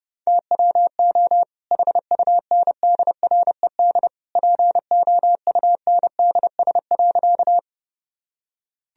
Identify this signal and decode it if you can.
Morse code, heard as a keyed tone, about 32 wpm, 700 Hz. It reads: TWO HUNDRED POUNDS.
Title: Sentences from Top 1000 Words - Review Part 2 - 30wpm